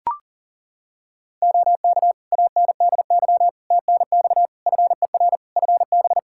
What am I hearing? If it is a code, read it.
Morse code, heard as a keyed tone, about 37 wpm, 700 Hz. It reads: OK ANDY TNX FER FB
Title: K1USN SST QSO Practice 40wpm